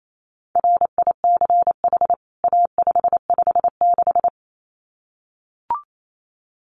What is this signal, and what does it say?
Morse code, heard as a keyed tone, about 28 wpm, 700 Hz. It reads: RICH A556
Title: CWT Exchange 28wpm